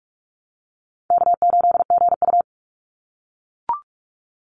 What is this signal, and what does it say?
Morse code, heard as a keyed tone, about 44 wpm, 700 Hz. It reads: K8GU
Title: Easy Call Signs - 45wpm